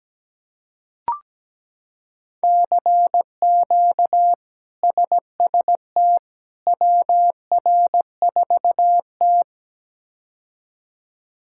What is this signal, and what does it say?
Morse code, heard as a keyed tone, about 17 wpm, 700 Hz. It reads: CQ SST WR4T
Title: K1USN SST QSO Practice - 17wpm